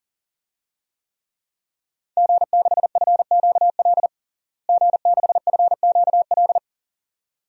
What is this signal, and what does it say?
Morse code, heard as a keyed tone, about 40 wpm, 700 Hz. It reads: G6FQL G6FQL
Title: DXCC-100 Call Signs - 5 Characters - Repeated 2x - 40wpm